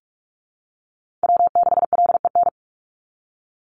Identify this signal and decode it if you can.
Morse code, heard as a keyed tone, about 44 wpm, 700 Hz. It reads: W6LEN
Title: Call Signs Repeated 3x - 45wpm